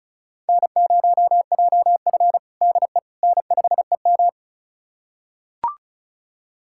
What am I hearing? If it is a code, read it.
Morse code, heard as a keyed tone, about 34 wpm, 690 Hz. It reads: N0JF DE N5EM
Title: USA Callsign DE Callsign - 4 Characters - 35wpm